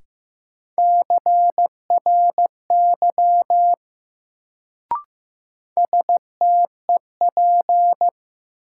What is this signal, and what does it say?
Morse code, heard as a keyed tone, about 16 wpm, 700 Hz. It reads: CRY STEP
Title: Sets of 5 Words Encore - New Word Drill - 15wpm